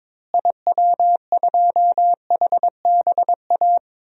Morse code, heard as a keyed tone, about 22 wpm, 700 Hz. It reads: IW2HBA
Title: DXCC-100 Call Signs - 6 Characters - Repeated 3x - 22wpm